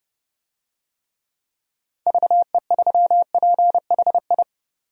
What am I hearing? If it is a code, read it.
Morse code, heard as a keyed tone, about 30 wpm, 700 Hz. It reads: VE3PHI
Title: DXCC-100 Call Signs - 6 Characters - 30wpm